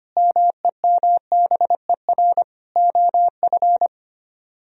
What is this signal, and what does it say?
Morse code, heard as a keyed tone, about 33 wpm, 700 Hz. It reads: MEMBER OF